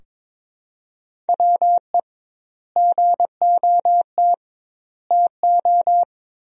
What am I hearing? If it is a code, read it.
Morse code, heard as a keyed tone, about 21 wpm, 700 Hz. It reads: WE GOT TO
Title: 2x Word-Spacing - Sentences from Top 300 Words - 22wpm